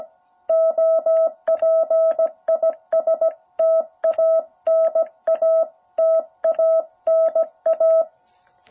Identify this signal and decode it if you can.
Morse code, heard as a keyed tone, about 17 wpm, 660 Hz. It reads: OP IS TANA TANA
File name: jh8cbh / 2021cw